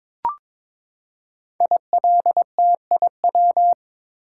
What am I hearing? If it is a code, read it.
Morse code, heard as a keyed tone, about 22 wpm, 700 Hz. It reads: ILTIW
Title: Code Groups - 5 Letters - 22wpm